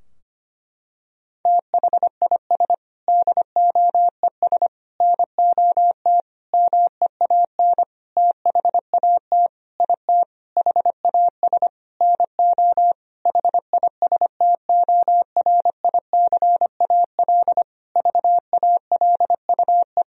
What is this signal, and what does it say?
Morse code, heard as a keyed tone, about 25 wpm, 700 Hz. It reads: THIS DOES NOT MEAN THAT IT HAS NO HISTORICAL VALUE
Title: Sentences from Top 1000 Words - Review Part 4 - 25wpm